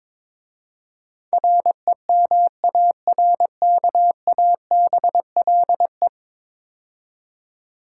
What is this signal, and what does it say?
Morse code, heard as a keyed tone, about 22 wpm, 700 Hz. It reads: REMARKABLE
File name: practice / 22wpm / 10 Letter Words - First 3 Letters Given - 22wpm